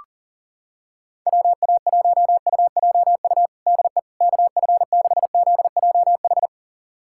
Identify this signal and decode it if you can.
Morse code, heard as a keyed tone, about 37 wpm, 700 Hz. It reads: WA1UJU DE KF6ZJH
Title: USA Callsign DE Callsign 40wpm